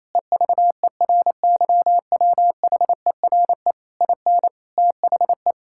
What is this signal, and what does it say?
Morse code, heard as a keyed tone, about 28 wpm, 700 Hz. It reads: EVERYWHERE IN THE
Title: Sentences from Top 1600 Words 28wpm